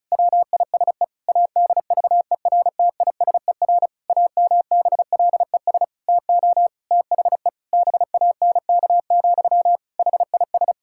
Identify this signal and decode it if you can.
Morse code, heard as a keyed tone, about 36 wpm, 700 Hz. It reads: WISE ADVERTISER AMBLES TO THE BANK, HIS
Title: Pangram Challenge v2 - 35wpm